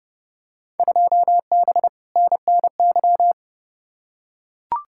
Morse code, heard as a keyed tone, about 30 wpm, 700 Hz. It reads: 2B NNY
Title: Field Day Exchanges Encore - 30wpm